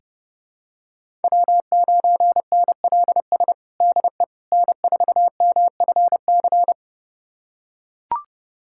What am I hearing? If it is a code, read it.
Morse code, heard as a keyed tone, about 30 wpm, 700 Hz. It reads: W9NLS DE N4MFC
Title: USA Callsign DE Callsign 30wpm